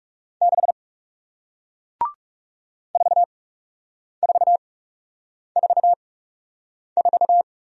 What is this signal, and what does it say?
Morse code, heard as a keyed tone, about 39 wpm, spaced as for 8 wpm, 700 Hz. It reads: 6 4444